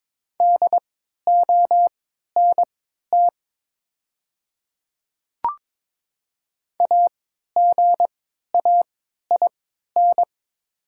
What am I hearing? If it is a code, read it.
Morse code, heard as a keyed tone, about 23 wpm, spaced as for 13 wpm, 690 Hz. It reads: DONT AGAIN
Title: 3x Character Spacing - Top 300 Words - 22wpm